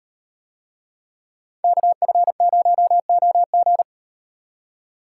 Morse code, heard as a keyed tone, about 38 wpm, 700 Hz. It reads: KF0OG